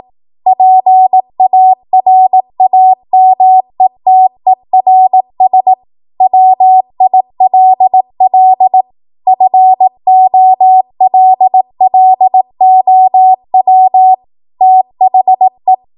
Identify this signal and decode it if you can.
Morse code, heard as a keyed tone, about 18 wpm, 750 Hz. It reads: PARAMETERS WILL FOLLOW THE
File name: Morse / 18WPM